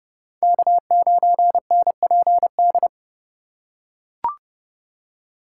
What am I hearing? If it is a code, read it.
Morse code, heard as a keyed tone, about 32 wpm, 700 Hz. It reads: K9NPD
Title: USA Call Signs - 5 Characters - 2x Character Spacing - 30wpm